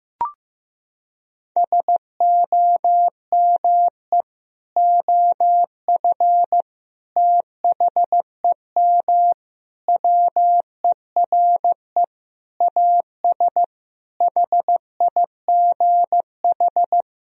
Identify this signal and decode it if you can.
Morse code, heard as a keyed tone, about 15 wpm, 700 Hz. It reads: SOME OF THEM WERE AS HIGH
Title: Sentences from Top 1000 Words - Review Part 1 - 15wpm